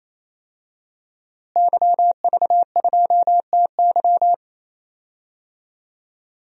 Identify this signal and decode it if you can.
Morse code, heard as a keyed tone, about 28 wpm, 700 Hz. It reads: YV2TY
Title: DXCC-100 Call Signs - 5 Characters - Repeated 3x - 28wpm